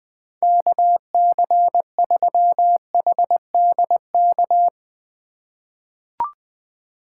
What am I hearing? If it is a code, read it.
Morse code, heard as a keyed tone, about 20 wpm, 700 Hz. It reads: KC3HDK
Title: USA Call Signs - 6 Characters - 3x Character Spacing - 20wpm